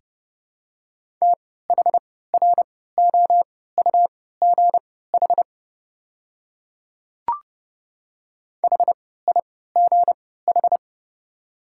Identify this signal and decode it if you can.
Morse code, heard as a keyed tone, about 30 wpm, spaced as for 17 wpm, 700 Hz. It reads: THROUGH HIGH
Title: 3x Character Spacing - Top 200 Words - 30wpm